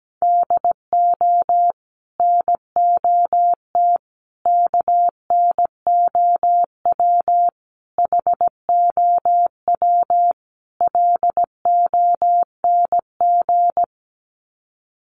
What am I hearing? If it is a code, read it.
Morse code, heard as a keyed tone, about 17 wpm, 700 Hz. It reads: DO NOT KNOW HOW LONG
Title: Sets of 5 Words Encore2 17wpm